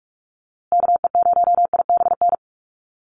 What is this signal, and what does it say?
Morse code, heard as a keyed tone, about 44 wpm, 700 Hz. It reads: KE0IBN
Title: USA Call Signs Repeated 3x 45wpm